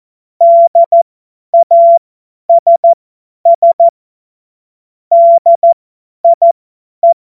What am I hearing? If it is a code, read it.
Morse code, heard as a keyed tone, about 14 wpm, spaced as for 10 wpm, 680 Hz. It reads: DASS DIE